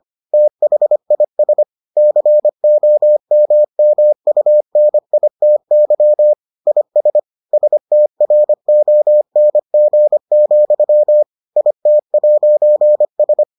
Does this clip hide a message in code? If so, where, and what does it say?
Yes: Morse code from 0.3 to 13.4 s, heard as a keyed tone, about 25 wpm, 600 Hz. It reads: THIS COMMUNITY IS STRONG, IT'S